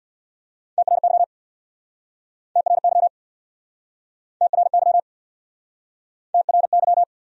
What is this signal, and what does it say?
Morse code, heard as a keyed tone, about 67 wpm, 700 Hz. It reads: TRY TRY TRY TRY